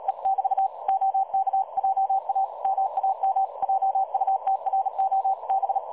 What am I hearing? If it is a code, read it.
Morse code, heard as a keyed tone, about 36 wpm, 770 Hz, with radio noise. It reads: E <BT> OK JA MAM JUTRO B
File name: dlazaawansowanych / nagraniazpasma / websdr recording 3565.0kHz